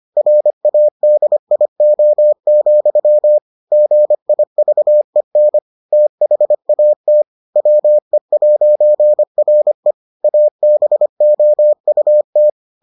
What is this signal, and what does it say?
Morse code, heard as a keyed tone, about 25 wpm, 600 Hz. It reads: RADIO, GIVEN THAT WE'RE ABOUT